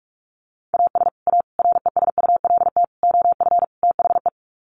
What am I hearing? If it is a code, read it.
Morse code, heard as a keyed tone, about 44 wpm, 700 Hz. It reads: AS A RESULT OF THE
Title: Sets of 5 Words 45wpm